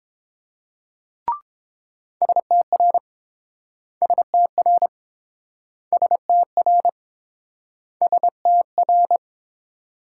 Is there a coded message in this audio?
Yes: Morse code from 2.2 to 9.2 s, heard as a keyed tone, about 29 wpm, 700 Hz. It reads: STR STR STR STR